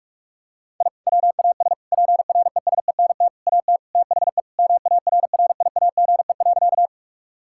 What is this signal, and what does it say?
Morse code, heard as a keyed tone, about 46 wpm, 700 Hz. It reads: I WAS PRESENT AT THE MARRIAGE.
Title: Sentences from Top 1000 Words - Review Part 5 - 45wpm